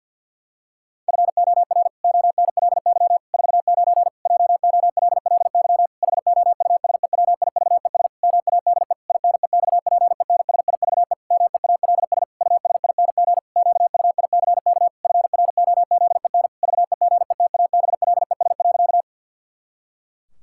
Morse code, heard as a keyed tone, about 50 wpm, 700 Hz. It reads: FOR ONLY 49 JOLLY HOUSEWIVES MADE INEXPENSIVE MEALS USING QUICK FROZEN VEGETABLES.
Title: Pangram Challenge v2 - 50wpm